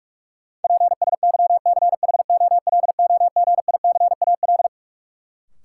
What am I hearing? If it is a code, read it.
Morse code, heard as a keyed tone, about 46 wpm, 700 Hz. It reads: PSYCHOLOGICAL
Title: Sentences from Top 1400 Words - New Word Drill Alt - 45wpm